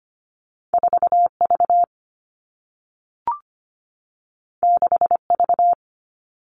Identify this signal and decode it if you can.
Morse code, heard as a keyed tone, about 25 wpm, 700 Hz. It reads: 4V 6V
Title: Two Challenging Letter-Number Combinations 25wpm